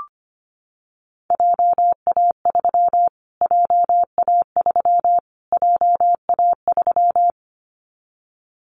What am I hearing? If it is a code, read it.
Morse code, heard as a keyed tone, about 25 wpm, 700 Hz. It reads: JA3 JA3 JA3
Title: Callsign Prefixes for Top 25 DXCC - Repeated 3x - 25wpm